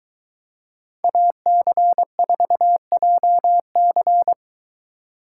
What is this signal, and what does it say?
Morse code, heard as a keyed tone, about 23 wpm, 700 Hz. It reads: AC4JC